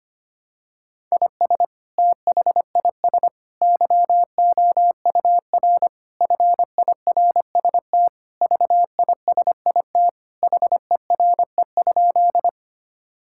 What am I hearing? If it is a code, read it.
Morse code, heard as a keyed tone, about 25 wpm, 700 Hz. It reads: IS THIS YOUR FIRST VISIT HERE?
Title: Sentences from Top 1000 Words - Review Part 6 - 25wpm